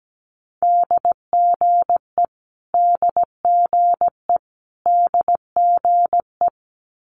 Morse code, heard as a keyed tone, about 17 wpm, 700 Hz. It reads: DGE DGE DGE
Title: Multi-letter Phonemes Repeated 3x 17wpm